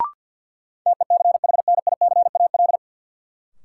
Morse code, heard as a keyed tone, about 48 wpm, 700 Hz. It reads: TECHNICAL
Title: Sentences from Top 1100 Words - New Word Drill - 50wpm